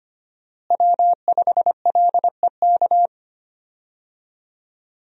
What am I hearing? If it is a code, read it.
Morse code, heard as a keyed tone, about 25 wpm, 700 Hz. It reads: W5LEK
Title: USA Call Signs - 5 Characters - Repeated 3x - 25wpm